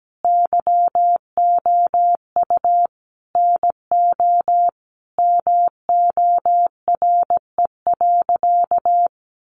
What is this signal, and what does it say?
Morse code, heard as a keyed tone, about 17 wpm, 700 Hz. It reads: YOU NO MORE.
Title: Sentences from Top 1200 Words 17wpm